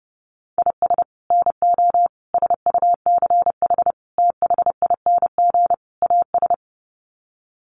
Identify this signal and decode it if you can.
Morse code, heard as a keyed tone, about 30 wpm, 700 Hz. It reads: IS NO SUCH THING AS
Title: Sets of 5 Words Encore 30wpm